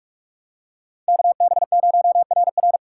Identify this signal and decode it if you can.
Morse code, heard as a keyed tone, about 43 wpm, 700 Hz. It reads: KB0RR
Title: USA Call Signs - 5 Characters - 4x Character Spacing - 45wpm